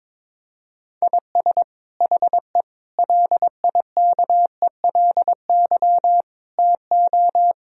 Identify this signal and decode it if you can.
Morse code, heard as a keyed tone, about 21 wpm, 700 Hz. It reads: IS HE LIKELY TO
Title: Sentences from Top 1000 Words - Review Part 4 - 22wpm